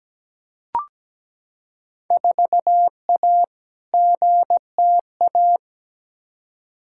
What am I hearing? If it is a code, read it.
Morse code, heard as a keyed tone, about 17 wpm, 700 Hz. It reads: 4A GTA
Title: Field Day Exchanges Encore - 17wpm